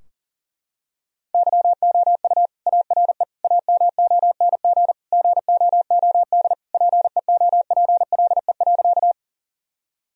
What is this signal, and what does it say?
Morse code, heard as a keyed tone, about 40 wpm, 700 Hz. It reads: YOU ARE AMONG GOOD PEOPLE.
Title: Sentences from Top 1000 Words - Review Part 3 - 40wpm